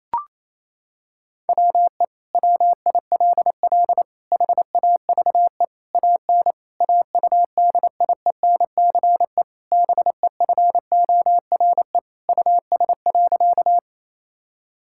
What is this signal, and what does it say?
Morse code, heard as a keyed tone, about 28 wpm, 700 Hz. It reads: WE WILL HAVE AN AUDIENCE BEFORE US.
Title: Sentences from Top 1000 Words - Review Part 5 - 28wpm